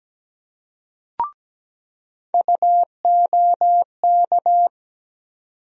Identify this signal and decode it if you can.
Morse code, heard as a keyed tone, about 17 wpm, 700 Hz. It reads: UOK
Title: USA Call Sign Postfixes - 3 Letters 17wpm